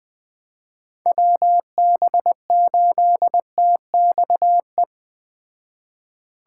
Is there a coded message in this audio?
Yes: Morse code from 1.1 to 4.8 s, heard as a keyed tone, about 20 wpm, 700 Hz. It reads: WB8TXE